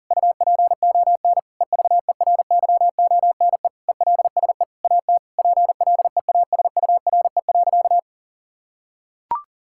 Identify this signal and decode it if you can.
Morse code, heard as a keyed tone, about 40 wpm, 700 Hz. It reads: UPON EVERYONE ELSE AT PLEASURE.
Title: Sentences from Top 1600 Words 40wpm